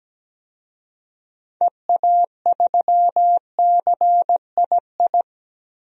Morse code, heard as a keyed tone, about 17 wpm, 700 Hz. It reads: EA3CII
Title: DXCC-100 Call Signs - 6 Characters - Repeated 2x - 17wpm